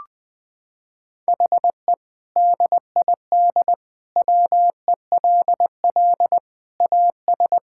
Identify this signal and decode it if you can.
Morse code, heard as a keyed tone, about 21 wpm, 700 Hz. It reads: HE DID WELL AS
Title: Sentences with 2-Word Phrases - 20wpm